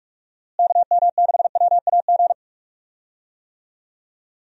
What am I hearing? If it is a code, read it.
Morse code, heard as a keyed tone, about 46 wpm, 700 Hz. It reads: KM6WAG
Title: USA Call Signs - 6 Characters - Repeated 3x - 45wpm